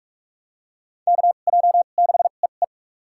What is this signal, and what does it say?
Morse code, heard as a keyed tone, about 46 wpm, spaced as for 33 wpm, 700 Hz. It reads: KJ6EE
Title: USA Call Signs - 5 Characters - 2x Character Spacing - 45wpm